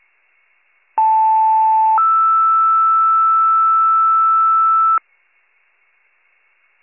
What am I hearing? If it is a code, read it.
Two-tone page, heard as a one-second tone, then a three-second tone, about 880 then 1320 Hz. It is heard from 1.0 to 5.0 s.